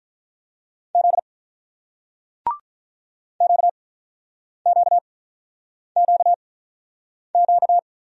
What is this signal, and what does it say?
Morse code, heard as a keyed tone, about 47 wpm, spaced as for 9 wpm, 700 Hz. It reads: Z QQQQ